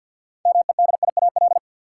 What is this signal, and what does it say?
Morse code, heard as a keyed tone, about 53 wpm, 700 Hz. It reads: MEDIAL
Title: Practice QSO - Exchange 3 - More Info - 50wpm